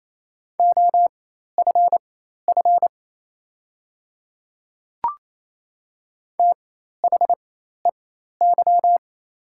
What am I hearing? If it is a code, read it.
Morse code, heard as a keyed tone, about 28 wpm, spaced as for 13 wpm, 690 Hz. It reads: OFF THEY
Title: 4x Character Spacing - Top 300 Words - 28wpm